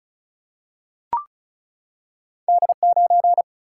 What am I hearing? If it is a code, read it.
Morse code, heard as a keyed tone, about 36 wpm, 700 Hz. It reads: D9
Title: Single-Letter Single-Number 35wpm